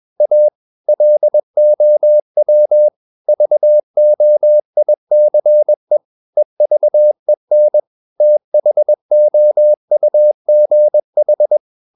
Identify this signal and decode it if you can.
Morse code, heard as a keyed tone, about 21 wpm, 600 Hz. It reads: A LOW VOICE EVEN THOUGH
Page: CW med Gnister